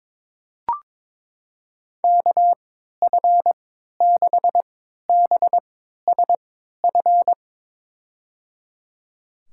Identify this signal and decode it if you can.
Morse code, heard as a keyed tone, about 22 wpm, spaced as for 13 wpm, 700 Hz. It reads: KF6BSF
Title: USA Call Signs - 3x Character Spacing - 22wpm